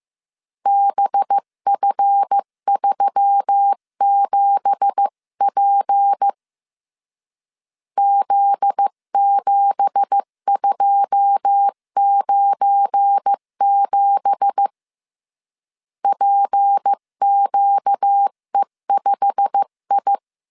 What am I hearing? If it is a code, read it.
Morse code, heard as a keyed tone, about 15 wpm, 800 Hz. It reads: BF37P Z7297 PQE5I
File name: CW / DET / 1 9 1500C S B 14wpm 3302